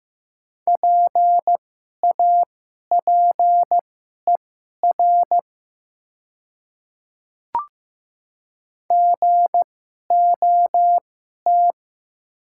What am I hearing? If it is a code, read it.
Morse code, heard as a keyed tone, about 15 wpm, spaced as for 11 wpm, 700 Hz. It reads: PAPER GOT
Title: 2x Character Spacing - Top 300 Words - 15wpm